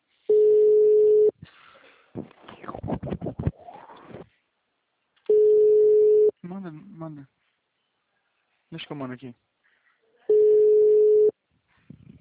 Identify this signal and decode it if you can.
Ringback tone (Europe, 425 Hz), with speech.